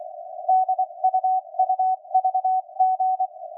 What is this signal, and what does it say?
Morse code, heard as a keyed tone, about 24 wpm, 750 Hz. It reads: DUUVG